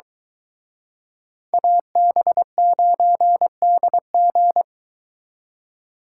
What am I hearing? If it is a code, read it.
Morse code, heard as a keyed tone, about 23 wpm, 700 Hz. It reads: AB9DG